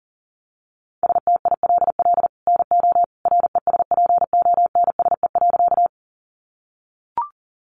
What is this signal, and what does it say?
Morse code, heard as a keyed tone, about 40 wpm, 700 Hz. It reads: STILL NO RESPONSE.
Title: Sentences from Top 1400 Words 40wpm